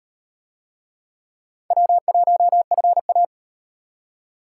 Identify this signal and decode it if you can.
Morse code, heard as a keyed tone, about 35 wpm, 700 Hz. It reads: W1FA